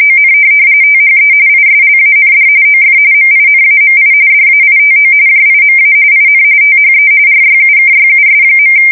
RTTY, about 45.45 baud. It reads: ON THE ISLAND OF MINAMI DAITO JIMA IN THE RYUKYUS.
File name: fsk170 / USS-Massachusetts